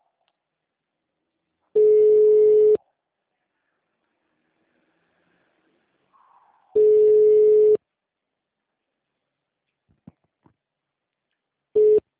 Ringback tone (Europe, 425 Hz).
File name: out- -2046- - - .451